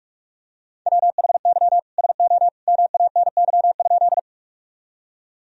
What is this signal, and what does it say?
Morse code, heard as a keyed tone, about 45 wpm, 700 Hz. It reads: WHY SO MANY?
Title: Sentences from Top 1000 Words - Review Part 1 - 45wpm